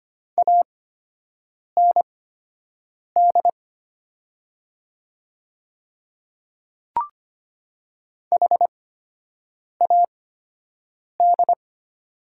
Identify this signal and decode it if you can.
Morse code, heard as a keyed tone, about 25 wpm, spaced as for 7 wpm, 700 Hz. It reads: AND HAD
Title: 8x Character Spacing - Top 200 Words - 25wpm